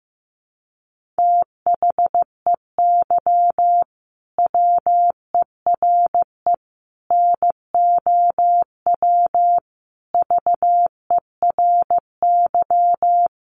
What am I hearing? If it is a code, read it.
Morse code, heard as a keyed tone, about 14 wpm, 700 Hz. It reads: THEY WERE NOW VERY
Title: Sentences from Top 1400 Words 15wpm